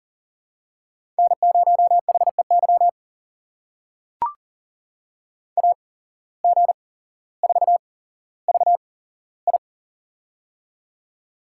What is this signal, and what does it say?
Morse code, heard as a keyed tone, about 37 wpm, spaced as for 17 wpm, 700 Hz. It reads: N0HEY AG4VI
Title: USA Call Signs - 5 Characters - 8x Character Spacing - 40wpm